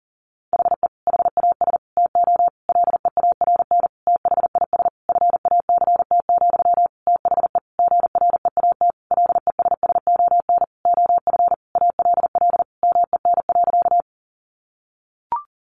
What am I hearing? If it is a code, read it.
Morse code, heard as a keyed tone, about 40 wpm, 700 Hz. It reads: HE HAS TO LEARN THIS FACT, THE GREAT LESSON OF ALL MEN.
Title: Sentences from Top 1200 Words 40wpm